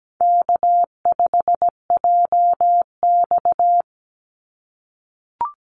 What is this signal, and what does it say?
Morse code, heard as a keyed tone, about 17 wpm, 700 Hz. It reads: K5JX
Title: Call Signs - 17wpm